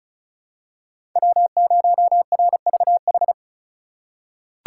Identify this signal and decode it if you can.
Morse code, heard as a keyed tone, about 36 wpm, 700 Hz. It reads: W0RVH
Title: USA Call Signs - 5 Characters - 4x Character Spacing - 35wpm